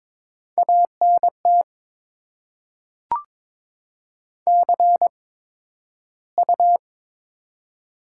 Morse code, heard as a keyed tone, about 22 wpm, spaced as for 12 wpm, 700 Hz. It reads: ANT CU
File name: Top 20 QSO Elements - 8x Character Spacing - 22wpm